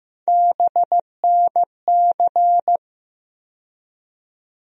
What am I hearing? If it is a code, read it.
Morse code, heard as a keyed tone, about 15 wpm, 700 Hz. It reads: BNC
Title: USA Call Sign Postfixes - 3 Letters 15wpm